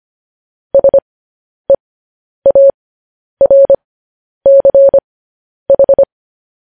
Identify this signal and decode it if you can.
Morse code, heard as a keyed tone, about 25 wpm, spaced as for 10 wpm, 550 Hz. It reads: SEARCH
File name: podcast / News headlines-1x-25x10